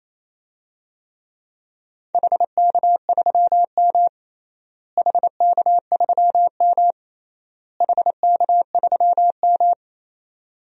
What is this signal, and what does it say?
Morse code, heard as a keyed tone, about 28 wpm, 700 Hz. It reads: HK3M HK3M HK3M
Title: DXCC-100 Call Signs - 4 Characters - Repeated 3x - 28wpm